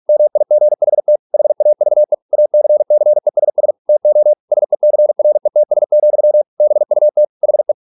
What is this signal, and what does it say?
Morse code, heard as a keyed tone, about 46 wpm, 600 Hz. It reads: MIGHT HAVE ACCESS TO SECRETS, BUT HE